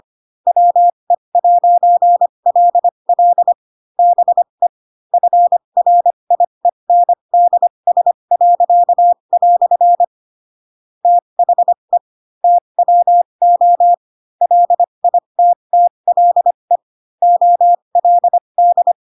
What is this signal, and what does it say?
Morse code, heard as a keyed tone, about 25 wpm, spaced as for 22 wpm, 700 Hz. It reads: WE'LL BE FRIENDS." THE TWO LITTLE OLD
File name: Pinocchio / Pinocchio ch2 18 25 ik1hge